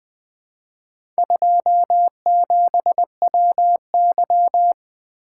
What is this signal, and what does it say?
Morse code, heard as a keyed tone, about 20 wpm, 700 Hz. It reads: 27WY
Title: Four Letter-Number Combinations - 20wpm